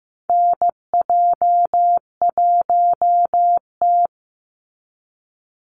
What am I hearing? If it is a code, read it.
Morse code, heard as a keyed tone, about 15 wpm, 700 Hz. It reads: NJ1T
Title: Call Signs Repeated 3x - 15wpm